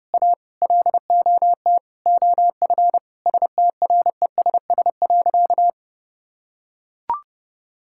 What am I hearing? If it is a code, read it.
Morse code, heard as a keyed tone, about 30 wpm, 700 Hz. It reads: A LOT OF STRESS.
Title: Sentences from Top 1000 Words - Review Part 6 - 30wpm